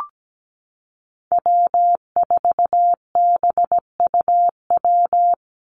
Morse code, heard as a keyed tone, about 17 wpm, 700 Hz. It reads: W4BUW
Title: Call Signs - 17wpm